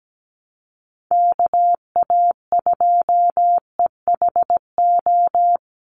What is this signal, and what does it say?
Morse code, heard as a keyed tone, about 17 wpm, 700 Hz. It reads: KA2EHO
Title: USA Call Signs 17wpm